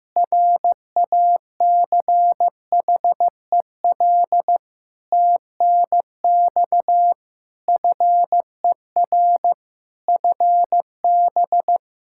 Morse code, heard as a keyed tone, about 16 wpm, 700 Hz. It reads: RACHEL TNX FER FB
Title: K1USN SST QSO Practice 15wpm